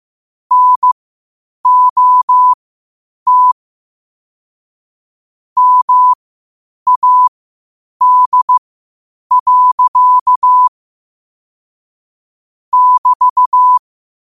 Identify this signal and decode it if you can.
Morse code, heard as a keyed tone, about 15 wpm, spaced as for 8 wpm, 1000 Hz. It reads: NOT MAD. <BT>
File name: qotdfiles / 8wpm